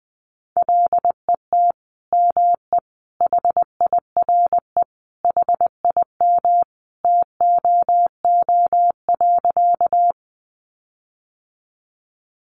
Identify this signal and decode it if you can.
Morse code, heard as a keyed tone, about 20 wpm, 700 Hz. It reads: LET ME HIRE HIM TOO.
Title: Sentences from Top 1400 Words 20wpm